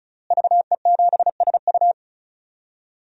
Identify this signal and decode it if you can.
Morse code, heard as a keyed tone, about 34 wpm, 700 Hz. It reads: VE7SU
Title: DXCC-100 Call Signs - 5 Characters - Repeated 2x - 35wpm